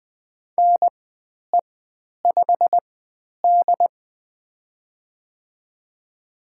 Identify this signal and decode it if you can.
Morse code, heard as a keyed tone, about 20 wpm, spaced as for 10 wpm, 700 Hz. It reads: NE5D
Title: 095 Lesson 16g - Letter c - Call Signs 20wpm